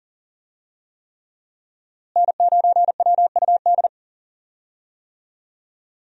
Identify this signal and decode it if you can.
Morse code, heard as a keyed tone, about 40 wpm, 700 Hz. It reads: N9WUD
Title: USA Call Signs - 5 Characters - Repeated 3x - 40wpm